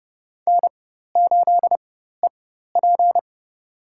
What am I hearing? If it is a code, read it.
Morse code, heard as a keyed tone, about 31 wpm, spaced as for 14 wpm, 700 Hz. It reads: N8EP
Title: USA Call Signs - 4 Characters - 4x Character Spacing - 30wpm